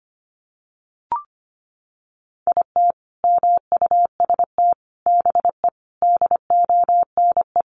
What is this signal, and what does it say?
Morse code, heard as a keyed tone, about 25 wpm, 700 Hz. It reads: IT MUST BE DONE
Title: Sentences from Top 1100 Words 25wpm